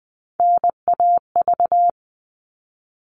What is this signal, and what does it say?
Morse code, heard as a keyed tone, about 20 wpm, 700 Hz. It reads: NAV
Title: All Cut Numbers - 3-Digit 20wpm